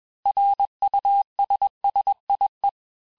Morse code, heard as a keyed tone, about 21 wpm, 800 Hz. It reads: RUSSIE
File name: texte-clair-vitesse-15mpm-puis-20mpm